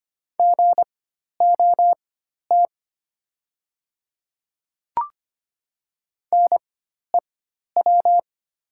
Morse code, heard as a keyed tone, about 25 wpm, spaced as for 12 wpm, 700 Hz. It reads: GOT NEW
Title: 4x Character Spacing - Top 300 Words - 25wpm